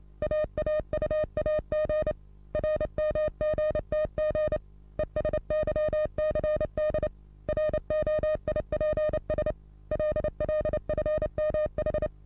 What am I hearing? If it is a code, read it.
Morse code, heard as a keyed tone, about 28 wpm, 610 Hz. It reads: AAUAG RMGTG ESYCD ROIPS LLFMH
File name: repos / Morsekurs / DJ4UF Morsen27